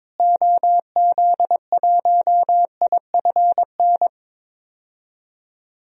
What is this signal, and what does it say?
Morse code, heard as a keyed tone, about 21 wpm, 700 Hz. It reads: OZ1IFN
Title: DXCC-100 Call Signs - 6 Characters - Repeated 3x - 22wpm